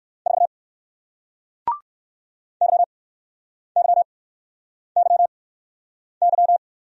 Morse code, heard as a keyed tone, about 54 wpm, spaced as for 9 wpm, 700 Hz. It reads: 4 YYYY
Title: Speed-Racing Single Letter-Number 45wpm